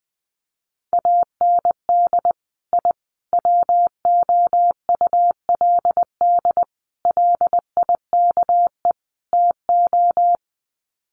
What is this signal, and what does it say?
Morse code, heard as a keyed tone, about 20 wpm, 700 Hz. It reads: AND I WOULD LIKE TO
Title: Sets of 5 Words Encore 20wpm